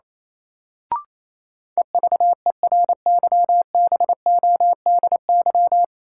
Morse code, heard as a keyed tone, about 28 wpm, 700 Hz. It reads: EVERYBODY
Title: Sentences from Top 1400 Words - New Word Drill Alt - 28wpm